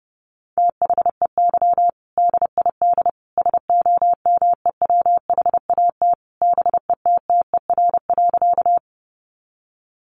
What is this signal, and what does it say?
Morse code, heard as a keyed tone, about 30 wpm, 700 Hz. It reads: THEY DID SOMEWHAT BETTER.
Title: Sentences from Top 1300 Words 30wpm